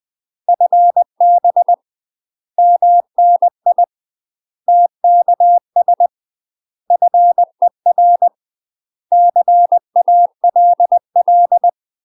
Morse code, heard as a keyed tone, about 20 wpm, 700 Hz. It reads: FB MNI TKS FER CALL
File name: UK MASTER QSOs TRAINING / 20WPM / UK MASTER QSOs 0060